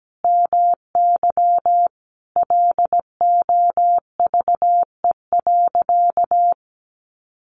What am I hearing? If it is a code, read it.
Morse code, heard as a keyed tone, about 17 wpm, 700 Hz. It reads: MY LOVE.
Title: Sentences from Top 1000 Words 17wpm